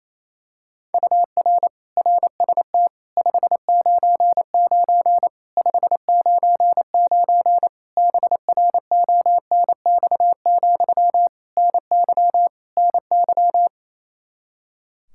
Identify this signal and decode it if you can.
Morse code, heard as a keyed tone, about 28 wpm, 700 Hz. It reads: UR RST 599 599 BRONX, NY NY
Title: Practice QSO - Exchange 1 - Beginning - 28wpm